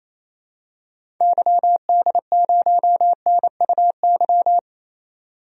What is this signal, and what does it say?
Morse code, heard as a keyed tone, about 27 wpm, 700 Hz. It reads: YD0NUY
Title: DXCC-100 Call Signs - 6 Characters - 28wpm